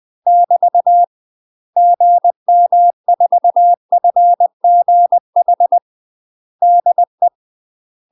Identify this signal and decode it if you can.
Morse code, heard as a keyed tone, about 20 wpm, 700 Hz. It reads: <BT> GM4FGH DE
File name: UK MASTER QSOs TRAINING/20WPM